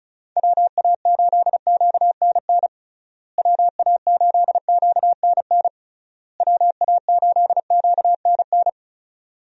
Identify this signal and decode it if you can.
Morse code, heard as a keyed tone, about 36 wpm, 700 Hz. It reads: WA8QNN WA8QNN WA8QNN
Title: USA Call Signs - 6 Characters - Repeated 3x - 35wpm